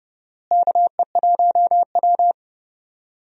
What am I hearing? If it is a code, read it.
Morse code, heard as a keyed tone, about 32 wpm, 700 Hz. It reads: KE1W
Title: USA Call Signs - 4 Characters - Repeated 3x - 30wpm